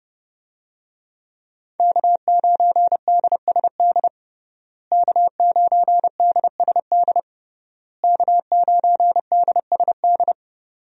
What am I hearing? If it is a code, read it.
Morse code, heard as a keyed tone, about 30 wpm, 700 Hz. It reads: K9DSD K9DSD K9DSD
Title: USA Call Signs - 5 Characters - Repeated 3x - 30wpm